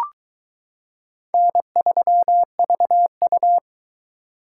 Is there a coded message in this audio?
Yes: Morse code from 1.3 to 3.6 s, heard as a keyed tone, about 21 wpm, 700 Hz. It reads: N3VU